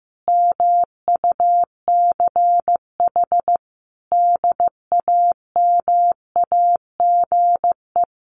Morse code, heard as a keyed tone, about 15 wpm, 700 Hz. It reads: MUCH DAMAGE
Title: Sentences from Top 1100 Words 15wpm